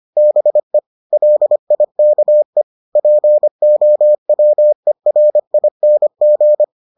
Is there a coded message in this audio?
Yes: Morse code from 0.2 to 6.6 s, heard as a keyed tone, about 25 wpm, 600 Hz. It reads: BE LIKE POWERING